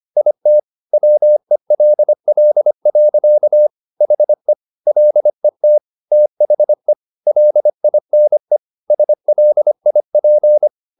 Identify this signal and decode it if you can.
Morse code, heard as a keyed tone, about 25 wpm, 600 Hz. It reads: IT WELL. HE LET THE LINE SLIP
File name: the old man and the sea part 1 25wpm